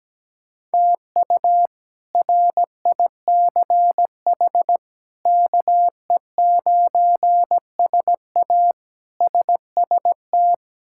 Morse code, heard as a keyed tone, about 17 wpm, 700 Hz. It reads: TU RICH KE9SA SST
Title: K1USN SST QSO Practice - 17wpm